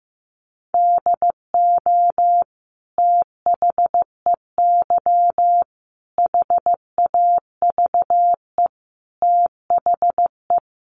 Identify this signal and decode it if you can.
Morse code, heard as a keyed tone, about 15 wpm, 700 Hz. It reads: DO THEY HAVE THE
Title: Sentences from Top 1300 Words 15wpm